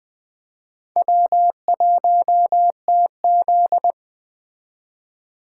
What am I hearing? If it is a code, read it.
Morse code, heard as a keyed tone, about 20 wpm, 700 Hz. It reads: W1TZ
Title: Speed Racing - USA 1x2 Call Signs 17wpm